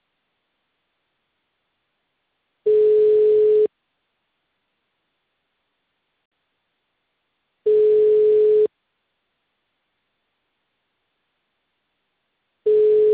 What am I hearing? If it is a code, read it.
Ringback tone (Europe, 425 Hz).
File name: out- -2006- - - .95